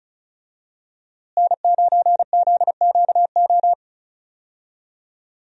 Morse code, heard as a keyed tone, about 34 wpm, 700 Hz. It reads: N9ZQO
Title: USA Call Signs - 5 Characters - Repeated 2x - 35wpm